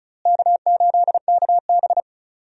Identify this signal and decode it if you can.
Morse code, heard as a keyed tone, about 34 wpm, 700 Hz. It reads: K8KB
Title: USA Call Signs - 4 Characters - Repeated 2x - 35wpm